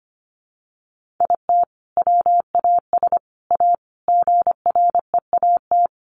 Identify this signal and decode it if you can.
Morse code, heard as a keyed tone, about 25 wpm, 700 Hz. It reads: IT WAS A GREAT
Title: Sentences from Top 1500 Words 25wpm